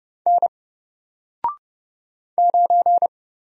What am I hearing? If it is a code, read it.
Morse code, heard as a keyed tone, about 30 wpm, 700 Hz. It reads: N 9
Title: Single Letter-Number 30wpm